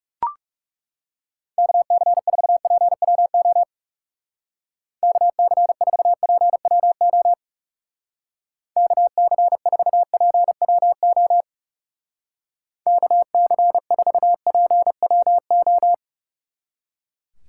Morse code, heard as a keyed tone, about 38 wpm, 700 Hz. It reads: KC4PWO KC4PWO KC4PWO KC4PWO